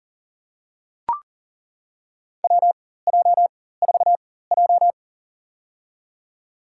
Morse code, heard as a keyed tone, about 40 wpm, spaced as for 19 wpm, 700 Hz. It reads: WJ4J
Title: USA Call Signs - 4x Character Spacing - 40wpm